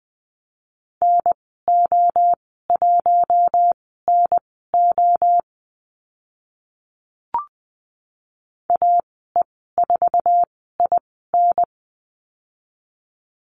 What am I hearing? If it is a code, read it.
Morse code, heard as a keyed tone, about 20 wpm, spaced as for 15 wpm, 700 Hz. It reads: NO1NO AE4IN
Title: 062 Lesson 11g - Letter h - Call Signs 20wpm